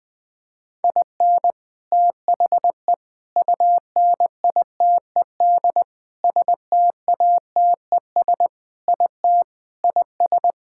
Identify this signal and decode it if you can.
Morse code, heard as a keyed tone, about 20 wpm, 700 Hz. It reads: IN THE UNITED STATES IT IS
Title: Sentences with 3-Word Phrases - 20wpm